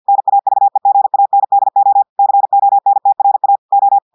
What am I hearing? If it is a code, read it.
Morse code, heard as a keyed tone, about 50 wpm, 800 Hz. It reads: NAVEGANDO CONTRA O
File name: audios / portugais / portugais texte en clair 1 vitesse 50 mots minutes